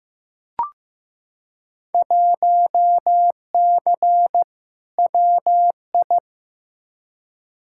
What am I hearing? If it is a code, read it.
Morse code, heard as a keyed tone, about 15 wpm, 700 Hz. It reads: 1C WI
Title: Field Day Exchanges - 15wpm